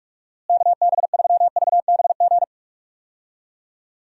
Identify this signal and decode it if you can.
Morse code, heard as a keyed tone, about 41 wpm, 700 Hz. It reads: KB3VBG
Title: USA Call Signs - 6 Characters - Repeated 3x - 45wpm